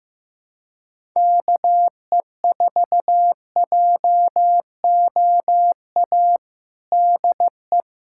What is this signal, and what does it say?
Morse code, heard as a keyed tone, about 15 wpm, 700 Hz. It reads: KE4JOA DE
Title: USA Callsign DE Callsign - 6 Characters - 15wpm